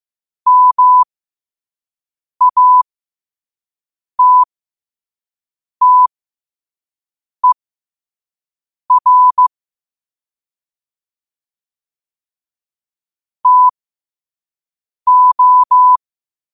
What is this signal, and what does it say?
Morse code, heard as a keyed tone, about 15 wpm, spaced as for 5 wpm, 1000 Hz. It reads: MATTER TO
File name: qotdfiles / 5wpm